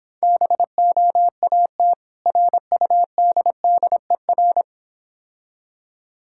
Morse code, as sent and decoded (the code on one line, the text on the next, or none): -... --- .- - / .-. ..- -.. -.. .-.
BOAT RUDDER